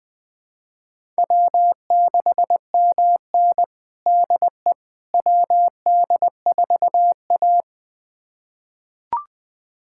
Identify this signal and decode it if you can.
Morse code, heard as a keyed tone, about 20 wpm, 700 Hz. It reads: W6MN DE WD4A
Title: USA Callsign DE Callsign - 4 Characters - 20wpm